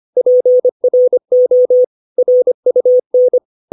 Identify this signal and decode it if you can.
Morse code, heard as a keyed tone, about 25 wpm, 500 Hz. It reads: PRO RUN
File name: 632 words / 632 Words 25WPM